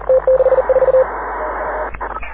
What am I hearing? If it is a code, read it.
Morse code, heard as a keyed tone, about 39 wpm, 520 Hz, with radio noise. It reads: T64
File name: AUDIO2019 / HG6N CW SSB MIX